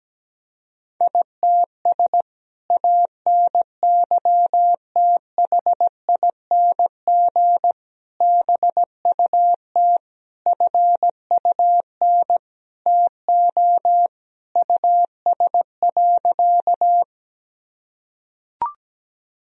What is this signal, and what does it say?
Morse code, heard as a keyed tone, about 17 wpm, 700 Hz. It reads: ITS ANYTHING BUT FUN TO US.
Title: Sentences from Top 1000 Words - Review Part 4 - 17wpm